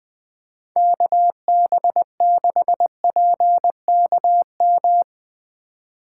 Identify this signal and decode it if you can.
Morse code, heard as a keyed tone, about 20 wpm, 700 Hz. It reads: KB6PKM
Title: USA Call Signs - 6 Characters - 2x Character Spacing - 20wpm